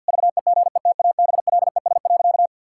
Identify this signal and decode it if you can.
Morse code, heard as a keyed tone, about 51 wpm, 700 Hz. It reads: VEGETABLES.
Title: Pangram Challenge v2 - 50wpm